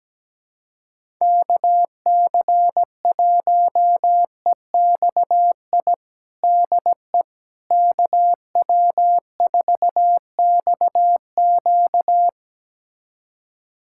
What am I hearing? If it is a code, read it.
Morse code, heard as a keyed tone, about 17 wpm, 700 Hz. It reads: KC1EXI DE KW4XQ
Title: USA Callsign DE Callsign 17wpm